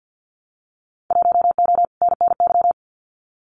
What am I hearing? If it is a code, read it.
Morse code, heard as a keyed tone, about 50 wpm, 700 Hz. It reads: 1O NNY
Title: Winter Field Day Exchanges Encore 50wpm